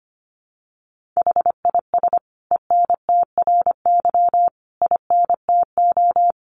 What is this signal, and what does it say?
Morse code, heard as a keyed tone, about 25 wpm, 700 Hz. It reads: HIS ENTRY INTO
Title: Sentences from Top 1500 Words 25wpm